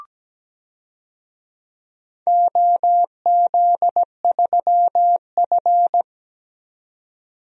Morse code, heard as a keyed tone, about 17 wpm, 700 Hz. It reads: OZ3F
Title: DXCC-100 Call Signs - 4 Characters - 17wpm